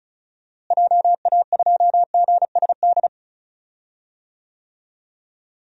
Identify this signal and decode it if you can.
Morse code, heard as a keyed tone, about 36 wpm, 700 Hz. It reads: JA2GSD
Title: DXCC-100 Call Signs - 6 Characters - Repeated 2x - 35wpm